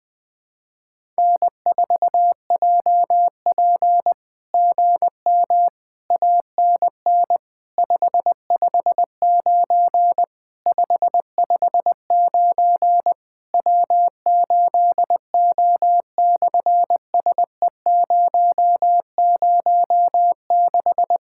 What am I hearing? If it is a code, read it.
Morse code, heard as a keyed tone, about 20 wpm, 700 Hz. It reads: N4JP GM ANN 559 559 W8O/SE006
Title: SOTA QSO Practice 20wpm